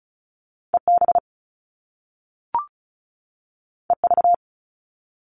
Morse code, heard as a keyed tone, about 36 wpm, 700 Hz. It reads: EB EV
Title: Top Two Letter Combinations in Words 35wpm